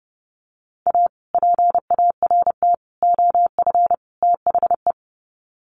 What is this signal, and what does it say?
Morse code, heard as a keyed tone, about 30 wpm, 700 Hz. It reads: A PART OF THE
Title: Sets of 4 Words 30wpm